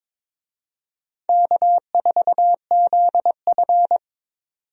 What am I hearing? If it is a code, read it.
Morse code, heard as a keyed tone, about 21 wpm, 700 Hz. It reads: K4ZF
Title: USA Call Signs - 4 Characters - 22wpm